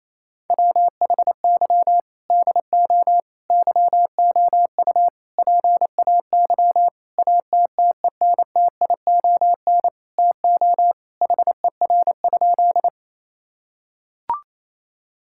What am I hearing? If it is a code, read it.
Morse code, heard as a keyed tone, about 28 wpm, 700 Hz. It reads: WHY DO YOU PAY ATTENTION TO HER?
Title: Sentences from Top 1000 Words - Review Part 6 - 28wpm